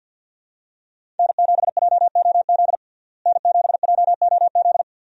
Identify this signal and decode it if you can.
Morse code, heard as a keyed tone, about 50 wpm, 700 Hz. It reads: N7JOZ N7JOZ
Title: USA Call Signs - 5 Characters - Repeated 2x - 50wpm